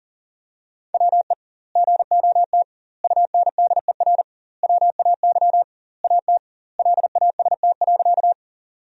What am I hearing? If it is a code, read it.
Morse code, heard as a keyed tone, about 37 wpm, 700 Hz. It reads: WE GOT UNDER WAY AT LAST.
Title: 2x Word-Spacing - Sentences from Top 300 Words - 40wpm